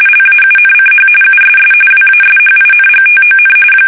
RTTY, about 45.45 baud. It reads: LAUNCH USS WASHINGTON
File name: fsk850 / battleshipnj